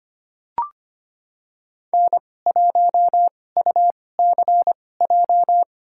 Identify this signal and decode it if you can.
Morse code, heard as a keyed tone, about 25 wpm, spaced as for 18 wpm, 700 Hz. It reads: N1UCJ
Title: USA Call Signs - 5 Characters - 2x Character Spacing - 25wpm